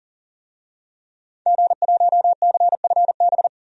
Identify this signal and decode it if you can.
Morse code, heard as a keyed tone, about 37 wpm, 700 Hz. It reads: G1CFB
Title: DXCC-100 Call Signs - 5 Characters - Repeated 2x - 40wpm